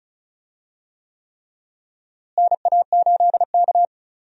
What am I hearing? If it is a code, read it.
Morse code, heard as a keyed tone, about 36 wpm, 700 Hz. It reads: NA8K
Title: USA Call Signs - 4 Characters - Repeated 3x - 35wpm